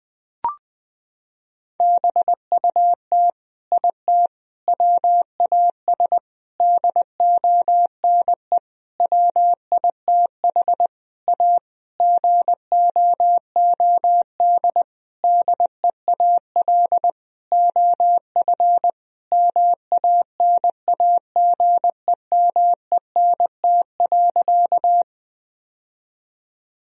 Morse code, heard as a keyed tone, about 21 wpm, 700 Hz. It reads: BUT IT WAS DONE WITH A GOOD DEAL OF MANAGEMENT.
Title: Sentences from Top 1000 Words - Review Part 4 - 20wpm